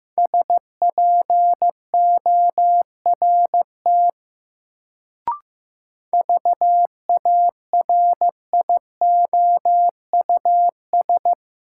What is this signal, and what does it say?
Morse code, heard as a keyed tone, about 15 wpm, 700 Hz. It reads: SPORT VARIOUS
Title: Sentences from Top 1000 Words - Review Part 4 - New Word Drill - 15wpm